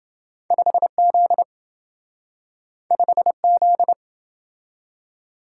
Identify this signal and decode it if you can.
Morse code, heard as a keyed tone, about 27 wpm, 700 Hz. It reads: 5Z 5Z